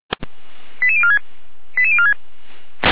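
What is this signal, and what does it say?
Five-tone selective call (ZVEI), heard as quick runs of five tones, tones of about 70 ms. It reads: 88046 88046
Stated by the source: ZVEI five-tone selective call received on an SDR.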